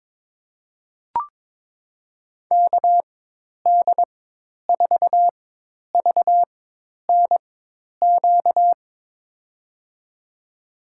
Morse code, heard as a keyed tone, about 22 wpm, spaced as for 10 wpm, 700 Hz. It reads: KD4VNQ
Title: USA Call Signs - 6 Characters - 4x Character Spacing - 22wpm